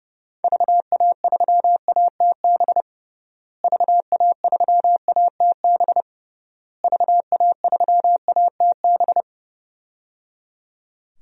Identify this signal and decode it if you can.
Morse code, heard as a keyed tone, about 30 wpm, 700 Hz. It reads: VA3ATB VA3ATB VA3ATB
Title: DXCC-100 Call Signs - 6 Characters - Repeated 3x - 30wpm